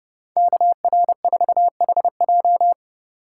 Morse code, heard as a keyed tone, about 30 wpm, 700 Hz. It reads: KR4HJ
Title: USA Call Signs - 5 Characters - 2x Character Spacing - 30wpm